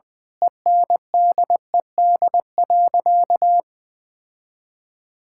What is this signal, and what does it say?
Morse code, heard as a keyed tone, about 20 wpm, 700 Hz. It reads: ENDED.
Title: Sentences with 2-Word Phrases - Spoken Beginning - 20wpm